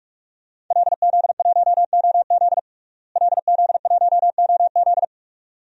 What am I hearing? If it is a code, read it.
Morse code, heard as a keyed tone, about 46 wpm, 700 Hz. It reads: LZ1OZ LZ1OZ
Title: DXCC-100 Call Signs - 5 Characters - Repeated 2x - 45wpm